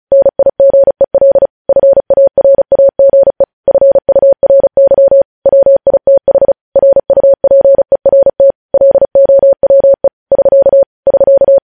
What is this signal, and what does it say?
Morse code, heard as a keyed tone, about 36 wpm, 550 Hz. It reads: NIGEL FARAGE FURY WITH RUPERT LOWE <SK> <SK>
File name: podcast / News headlines-1x-35x35